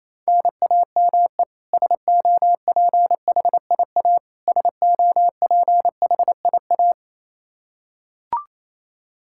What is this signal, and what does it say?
Morse code, heard as a keyed tone, about 28 wpm, 700 Hz. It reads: NAME SOPHIA SOPHIA
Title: Practice QSO - Exchange 1 - Beginning - 28wpm